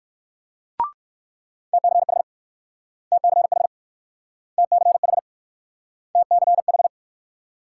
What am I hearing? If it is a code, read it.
Morse code, heard as a keyed tone, about 59 wpm, 700 Hz. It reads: TCH TCH TCH TCH